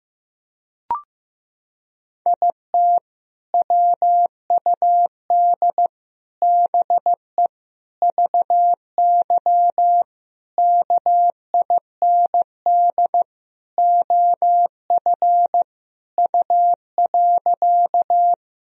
Morse code, heard as a keyed tone, about 15 wpm, 700 Hz. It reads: IT WUD BE VY KIND OF U.
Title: Sentences from Top 200 Words and Abbreviations 15wpm